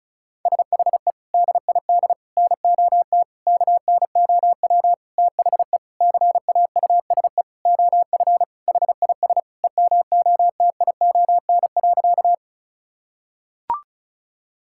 Morse code, heard as a keyed tone, about 34 wpm, 700 Hz. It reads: SHE DID NOT KNOW THE CAUSE OF HIS EMOTION.
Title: Sentences from Top 1600 Words 35wpm